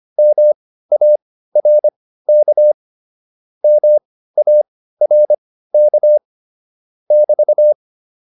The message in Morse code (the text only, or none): MARK MARK <BT>